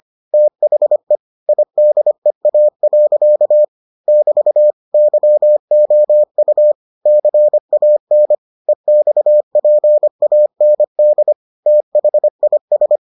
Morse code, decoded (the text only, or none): THE IDEA. <BT> YOU CAN EXPAND THIS